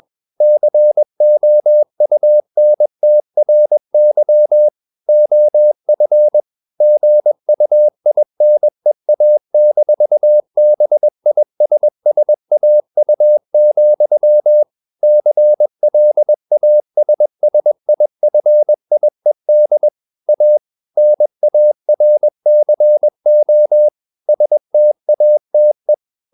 Morse code, heard as a keyed tone, about 21 wpm, 600 Hz. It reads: COUNTRY OF GUINEA-BISSAU, CLASSIFIED A NARCO STATE
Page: CW med Gnister